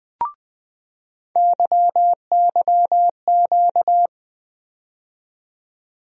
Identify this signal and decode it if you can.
Morse code, heard as a keyed tone, about 21 wpm, 700 Hz. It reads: YYQ
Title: USA Call Sign Postfixes - 3 Letters 20wpm